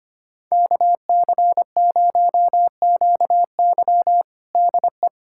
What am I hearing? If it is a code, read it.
Morse code, heard as a keyed tone, about 25 wpm, 700 Hz. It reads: KC0QY DE
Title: USA Callsign DE Callsign - 5 Characters - 25wpm